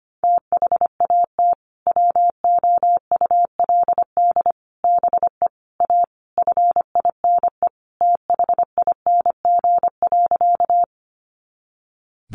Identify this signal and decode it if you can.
Morse code, heard as a keyed tone, about 25 wpm, 700 Hz. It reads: THAT WOULD BE A FINE THING.
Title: Sentences from Top 500 Words 25wpm